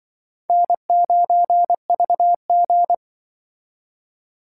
Morse code, heard as a keyed tone, about 25 wpm, 700 Hz. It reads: N9VG
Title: Speed Racing - USA 1x2 Call Signs 20wpm